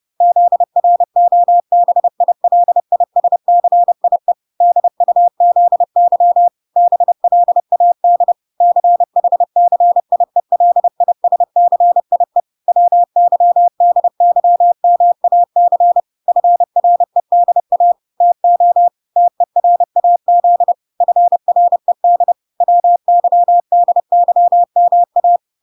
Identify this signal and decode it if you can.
Morse code, heard as a keyed tone, about 30 wpm, 700 Hz. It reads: ZROBILISCIE DUZY BLAD CHCIELISCIE WYDYMAC FREDA TO TERAZ FRED WYDYMA